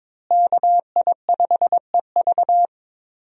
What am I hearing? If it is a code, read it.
Morse code, heard as a keyed tone, about 22 wpm, 700 Hz. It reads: KI5EV
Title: USA Call Signs - 5 Characters - Repeated 3x - 22wpm